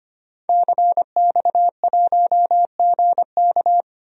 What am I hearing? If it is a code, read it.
Morse code, heard as a keyed tone, about 25 wpm, 700 Hz. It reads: CX1GK
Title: DXCC-100 Call Signs - 5 Characters - 25wpm